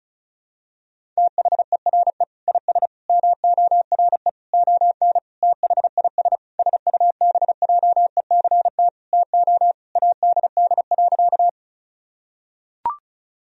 Morse code, heard as a keyed tone, about 34 wpm, 700 Hz. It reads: THERE IS MORE ON THIS SUBJECT TO ADD.
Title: Sentences from Top 1000 Words - Review Part 1 - 35wpm